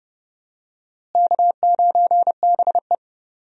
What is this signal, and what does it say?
Morse code, heard as a keyed tone, about 30 wpm, 700 Hz. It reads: K9BE
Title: USA Call Signs - 4 Characters - Repeated 3x - 30wpm